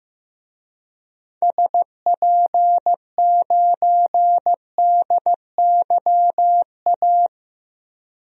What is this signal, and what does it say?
Morse code, heard as a keyed tone, about 15 wpm, 700 Hz. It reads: SP9DYA
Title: DXCC-100 Call Signs - 6 Characters - 15wpm